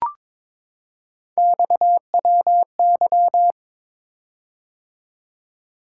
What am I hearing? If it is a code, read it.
Morse code, heard as a keyed tone, about 22 wpm, 700 Hz. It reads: XWY
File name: practice / 22wpm / Three Letter-Number Combinations - 22wpm